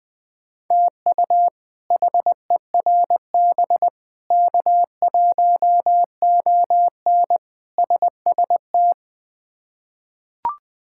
Morse code, heard as a keyed tone, about 20 wpm, 700 Hz. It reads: TU HERB K1ON SST
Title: K1USN SST QSO Practice - 20wpm